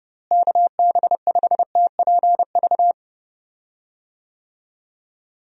Morse code, heard as a keyed tone, about 30 wpm, 700 Hz. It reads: KB5TPV
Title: USA Call Signs - 6 Characters - Repeated 3x - 30wpm